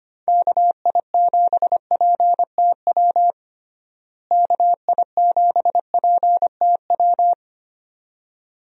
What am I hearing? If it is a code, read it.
Morse code, heard as a keyed tone, about 25 wpm, 700 Hz. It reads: KI7PTW KI7PTW
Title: USA Call Signs - 6 Characters - Repeated 2x - 25wpm